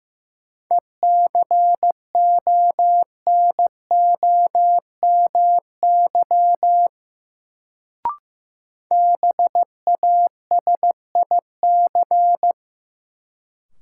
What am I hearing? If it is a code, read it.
Morse code, heard as a keyed tone, about 15 wpm, 700 Hz. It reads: ECONOMY BASIC
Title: Sentences from Top 1000 Words - Review Part 3 - New Word Drill - 15wpm